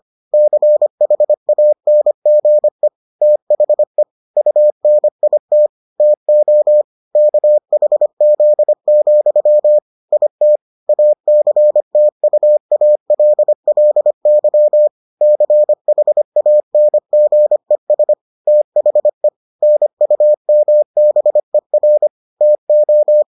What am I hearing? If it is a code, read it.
Morse code, heard as a keyed tone, about 25 wpm, 600 Hz. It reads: CHANGE THE UNIT TO KHZ, IT ACTUALLY CHANGES THE NUMBER TO